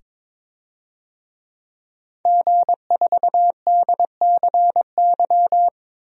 Morse code, heard as a keyed tone, about 22 wpm, 700 Hz. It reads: G4DCY
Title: DXCC-100 Call Signs - 5 Characters - 22wpm